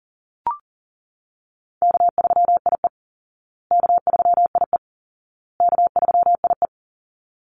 Morse code, heard as a keyed tone, about 40 wpm, 700 Hz. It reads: K3IE K3IE K3IE
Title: Easy Call Signs Repeated 3x - 40wpm